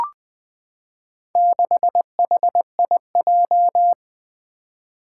Morse code, heard as a keyed tone, about 20 wpm, 700 Hz. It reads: 6HIJ
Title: Four Challenging Letter-Number Combinations 20wpm